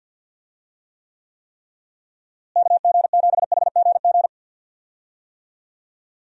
Morse code, heard as a keyed tone, about 50 wpm, 700 Hz. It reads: KG7HGG
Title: USA Call Signs - 6 Characters - Repeated 3x - 50wpm